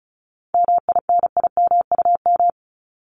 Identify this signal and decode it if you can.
Morse code, heard as a keyed tone, about 34 wpm, 700 Hz. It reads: MINIMUM
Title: Top 1251-1500 Words 35wpm